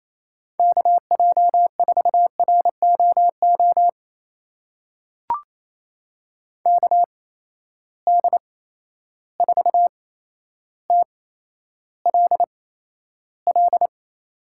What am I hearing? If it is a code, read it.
Morse code, heard as a keyed tone, about 28 wpm, spaced as for 12 wpm, 700 Hz. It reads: KJ4ROO KD4TLL
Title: USA Call Signs - 6 Characters - 8x Character Spacing - 28wpm